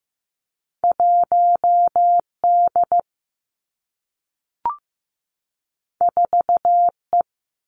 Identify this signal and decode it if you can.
Morse code, heard as a keyed tone, about 15 wpm, 700 Hz. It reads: 1 D 4 E
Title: Two Challenging Letter-Number Combinations 15wpm